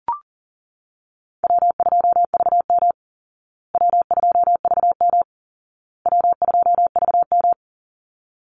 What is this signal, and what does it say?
Morse code, heard as a keyed tone, about 40 wpm, 700 Hz. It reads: W2VM W2VM W2VM
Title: Easy Call Signs Repeated 3x - 40wpm